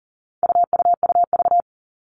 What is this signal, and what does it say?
Morse code, heard as a keyed tone, about 40 wpm, 700 Hz. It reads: UUUV
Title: All Cut Numbers - 4-Digit 40wpm